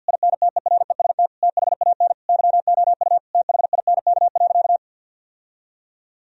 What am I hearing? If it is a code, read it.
Morse code, heard as a keyed tone, about 50 wpm, 700 Hz. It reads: INTEREST THAN YOU THINK.
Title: Sentences from Top 1000 Words - Review Part 2 - 50wpm